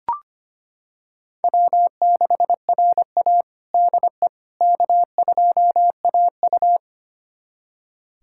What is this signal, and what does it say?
Morse code, heard as a keyed tone, about 25 wpm, 700 Hz. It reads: W6RA DE K2AU
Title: USA Callsign DE Callsign - 4 Characters - 25wpm